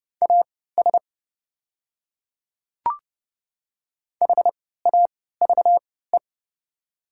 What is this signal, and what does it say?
Morse code, heard as a keyed tone, about 32 wpm, spaced as for 17 wpm, 700 Hz. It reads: AS HAVE
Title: 3x Character Spacing - Top 100 Words - 30wpm